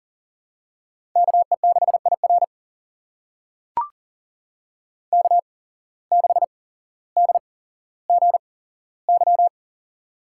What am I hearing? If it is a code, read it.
Morse code, heard as a keyed tone, about 40 wpm, spaced as for 17 wpm, 700 Hz. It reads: KE6IR K6DGY
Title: USA Call Signs - 5 Characters - 8x Character Spacing - 40wpm